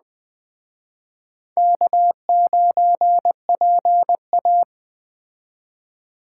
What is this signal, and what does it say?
Morse code, heard as a keyed tone, about 20 wpm, 700 Hz. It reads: K9PA
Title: Speed Racing - USA Call Signs 17wpm